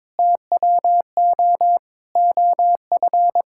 Morse code, heard as a keyed tone, about 22 wpm, 700 Hz. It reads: TWO OF
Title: Sentences from Top 1000 Words - Review Part 6 - 22wpm